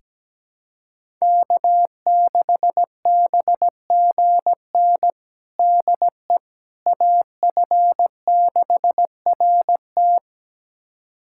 Morse code, heard as a keyed tone, about 17 wpm, 700 Hz. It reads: K6BGN DE AF6RT
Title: USA Callsign DE Callsign - 5 Characters - 17wpm